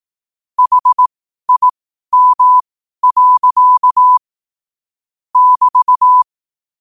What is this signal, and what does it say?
Morse code, heard as a keyed tone, about 18 wpm, spaced as for 12 wpm, 1000 Hz. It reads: HIM. <BT>
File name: qotdfiles / 12wpm